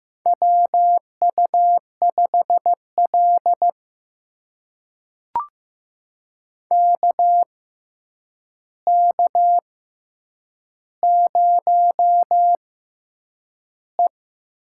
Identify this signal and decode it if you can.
Morse code, heard as a keyed tone, about 15 wpm, spaced as for 8 wpm, 700 Hz. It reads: WU5L KK0E
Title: USA Call Signs - 4 Characters - 6x Character Spacing - 15wpm